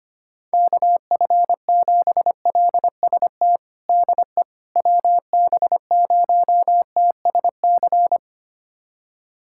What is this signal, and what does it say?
Morse code, heard as a keyed tone, about 25 wpm, 700 Hz. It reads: KF7LST DE WB0TSC
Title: USA Callsign DE Callsign 25wpm